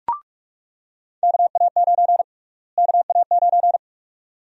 Morse code, heard as a keyed tone, about 47 wpm, 700 Hz. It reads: KA9 KA9
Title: USA Call Sign Prefixes - Repeated 2x - 45wpm